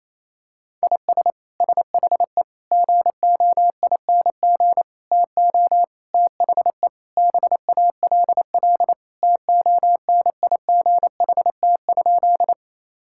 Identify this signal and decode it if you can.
Morse code, heard as a keyed tone, about 27 wpm, 700 Hz. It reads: IS SHE GOING TO THE BALL TONIGHT?
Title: Sentences from Top 1000 Words - Review Part 6 - 28wpm